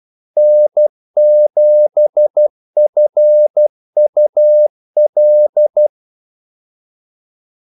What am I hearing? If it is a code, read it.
Morse code, heard as a keyed tone, about 12 wpm, 600 Hz. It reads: N7FUL
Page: 2 ekstra ordmellemrum mellem kaldesignalerne